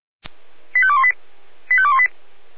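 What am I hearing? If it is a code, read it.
Five-tone selective call (ZVEI), heard as quick runs of five tones, tones of about 70 ms. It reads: 85218 85218
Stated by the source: ZVEI five-tone selective call 85878 received on SDR.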